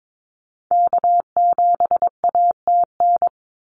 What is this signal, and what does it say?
Morse code, heard as a keyed tone, about 22 wpm, 700 Hz. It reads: K7ATN
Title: NA SOTA Activator Callsigns 22wpm